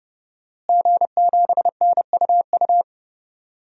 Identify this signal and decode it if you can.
Morse code, heard as a keyed tone, about 30 wpm, 700 Hz. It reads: G7NUU
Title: DXCC-100 Call Signs - 5 Characters - Repeated 2x - 30wpm